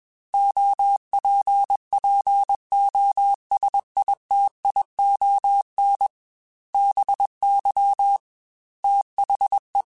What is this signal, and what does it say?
Morse code, heard as a keyed tone, about 21 wpm, 800 Hz. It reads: OPPOSITION BY THE
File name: Technical Subjects / Technical Subjects-mp3s / XC @20